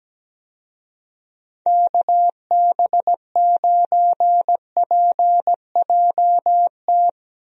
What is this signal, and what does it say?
Morse code, heard as a keyed tone, about 17 wpm, 700 Hz. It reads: KB9PJT
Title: USA Call Signs - 6 Characters - 17wpm